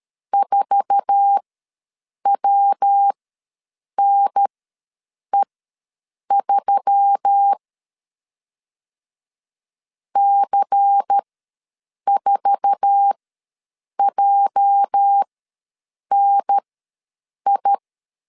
Morse code, heard as a keyed tone, about 13 wpm, spaced as for 7 wpm, 800 Hz. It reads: 4WNE3 C4JNI
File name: LEZIONI / 8 YJ45 / 8 YJ4 ws 12wpm 3249